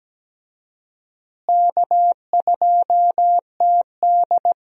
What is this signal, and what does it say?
Morse code, heard as a keyed tone, about 17 wpm, 700 Hz. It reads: K2TD
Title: USA Call Signs - 4 Characters - Repeated 2x - 17wpm